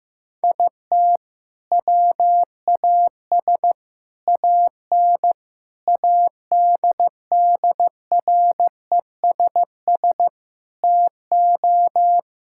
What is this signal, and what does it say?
Morse code, heard as a keyed tone, about 15 wpm, 700 Hz. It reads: IT WAS AN ADDRESS TO
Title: Sentences from Top 1000 Words - Review Part 6 - 15wpm